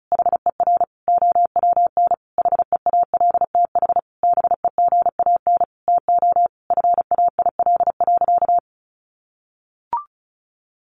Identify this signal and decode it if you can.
Morse code, heard as a keyed tone, about 34 wpm, 700 Hz. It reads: HER OWN HEALTH BEGAN TO FAIL.
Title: Sentences from Top 1100 Words 35wpm